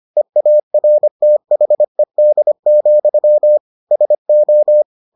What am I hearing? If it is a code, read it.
Morse code, heard as a keyed tone, about 25 wpm, 600 Hz. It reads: EARTHED, SO